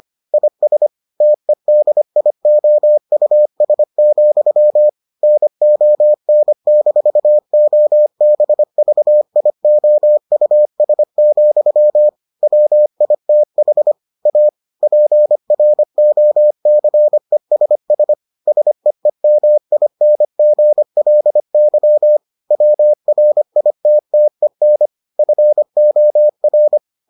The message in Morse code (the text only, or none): IS TEDIOUS, NON-OBVIOUS, WITH A PROCESS SEEMINGLY WRITTEN FOR